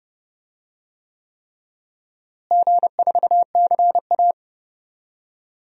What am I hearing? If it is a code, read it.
Morse code, heard as a keyed tone, about 30 wpm, 700 Hz. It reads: G4CA
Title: DXCC-100 Call Signs - 4 Characters - Repeated 3x - 30wpm